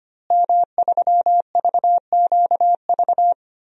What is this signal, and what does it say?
Morse code, heard as a keyed tone, about 25 wpm, 700 Hz. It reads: M3VQV
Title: DXCC-100 Call Signs - 5 Characters - Repeated 3x - 25wpm